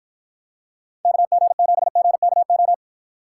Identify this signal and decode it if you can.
Morse code, heard as a keyed tone, about 54 wpm, 700 Hz. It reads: KG7GKO